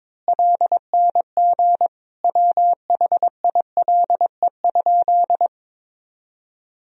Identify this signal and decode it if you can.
Morse code, heard as a keyed tone, about 22 wpm, 700 Hz. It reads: LNG WHILE?
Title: Sentences from Top 300 Words and Abbreviations 22wpm